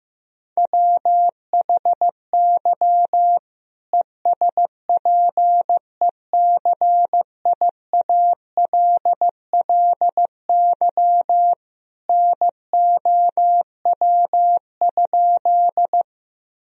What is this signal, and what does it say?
Morse code, heard as a keyed tone, about 15 wpm, 700 Hz. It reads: WHY ESPECIALLY NOW?
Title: Sentences from Top 1000 Words - Review Part 4 - 15wpm